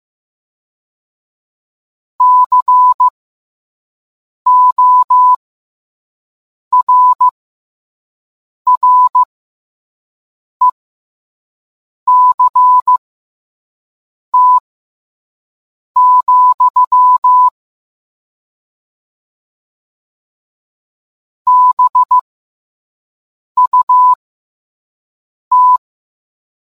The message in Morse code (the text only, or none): CORRECT, BUT